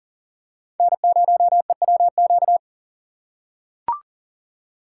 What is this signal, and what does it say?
Morse code, heard as a keyed tone, about 40 wpm, 700 Hz. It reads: N0EWQ
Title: USA Call Signs - 5 Characters - 6x Character Spacing - 40wpm